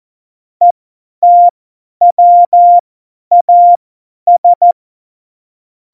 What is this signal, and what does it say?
Morse code, heard as a keyed tone, about 14 wpm, spaced as for 10 wpm, 700 Hz. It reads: ETWAS
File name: Morsekurs / Witze / Witz 25. Dezember, 50 F70